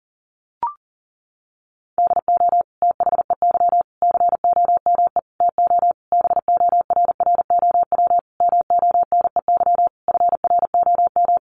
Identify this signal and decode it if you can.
Morse code, heard as a keyed tone, about 40 wpm, 700 Hz. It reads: DO THEY COME TO BORROW MONEY FROM
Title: Sentences from Top 1500 Words 40wpm